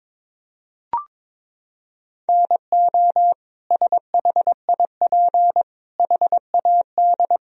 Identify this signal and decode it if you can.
Morse code, heard as a keyed tone, about 22 wpm, 700 Hz. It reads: NO SHIP HAD
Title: Sentences from Top 1000 Words - Review Part 4 - 22wpm